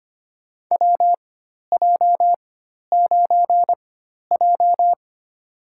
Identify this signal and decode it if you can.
Morse code, heard as a keyed tone, about 25 wpm, spaced as for 12 wpm, 700 Hz. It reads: WJ9J
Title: USA Call Signs - 4 Characters - 4x Character Spacing - 25wpm